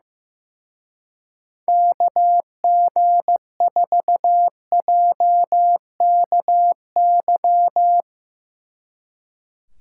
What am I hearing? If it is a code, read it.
Morse code, heard as a keyed tone, about 16 wpm, 700 Hz. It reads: KG4JKY
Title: Speed Racing - USA 2x3 Call Signs 15wpm